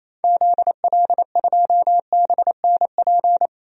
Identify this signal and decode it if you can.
Morse code, heard as a keyed tone, about 28 wpm, 700 Hz. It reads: ZL2BNP
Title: DXCC-100 Call Signs - 6 Characters - 28wpm